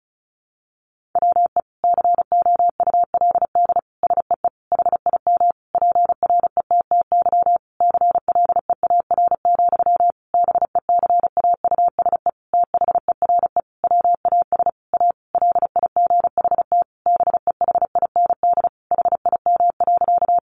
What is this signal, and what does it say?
Morse code, heard as a keyed tone, about 34 wpm, 700 Hz. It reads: WE COULD SEE HIM PRETTY CLEAR, BECAUSE THERE WAS A LIGHT BEHIND HIM.
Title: Sentences from Top 1000 Words 35wpm